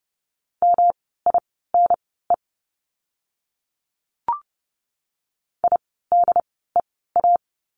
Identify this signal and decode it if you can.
Morse code, heard as a keyed tone, about 30 wpm, spaced as for 17 wpm, 700 Hz. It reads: MINE IDEA
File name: course / 30:10wpm / 108 Lesson 18f - Letter m - Words 30wpm